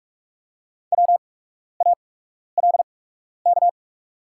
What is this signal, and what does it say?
Morse code, heard as a keyed tone, about 44 wpm, spaced as for 12 wpm, 700 Hz. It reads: WALK
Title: 8x Character Spacing - Top 300 Words - 45wpm